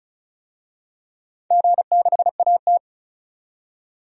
Morse code, heard as a keyed tone, about 36 wpm, 700 Hz. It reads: G6AT
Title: DXCC-100 Call Signs - 4 Characters - 35wpm